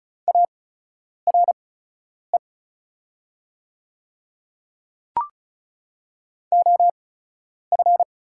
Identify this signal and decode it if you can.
Morse code, heard as a keyed tone, about 36 wpm, spaced as for 10 wpm, 690 Hz. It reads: ARE OF
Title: 8x Character Spacing - Top 100 Words - 35wpm